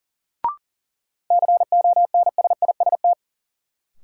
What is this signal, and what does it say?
Morse code, heard as a keyed tone, about 37 wpm, 700 Hz. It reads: CONSIST
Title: Sentences from Top 1400 Words - New Word Drill Alt - 40wpm